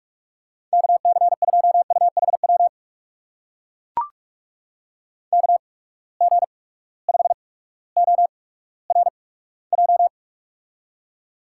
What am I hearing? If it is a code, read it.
Morse code, heard as a keyed tone, about 44 wpm, spaced as for 19 wpm, 700 Hz. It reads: KC2UHW KG5ORJ
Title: USA Call Signs - 6 Characters - 8x Character Spacing - 45wpm